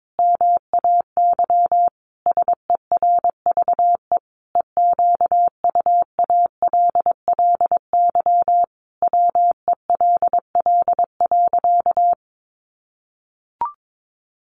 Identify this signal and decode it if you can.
Morse code, heard as a keyed tone, about 22 wpm, 700 Hz. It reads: MAY SERVE EQUALLY WELL.
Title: Sentences from Top 1400 Words 22wpm